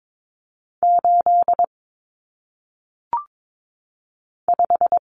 Morse code, heard as a keyed tone, about 22 wpm, 700 Hz. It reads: 8 5
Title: Numbers - 1-Digit 22wpm